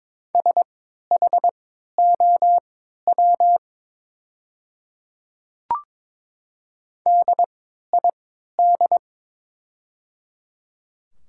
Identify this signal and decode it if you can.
Morse code, heard as a keyed tone, about 22 wpm, spaced as for 12 wpm, 690 Hz. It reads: SHOW DID
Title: 3x Character Spacing - Top 300 Words - 22wpm